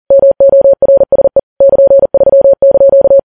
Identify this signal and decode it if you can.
Morse code, heard as a keyed tone, about 40 wpm, 550 Hz. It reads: MORSE <KN>3)
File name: Files Web tls / Morse on9fv / morse40kps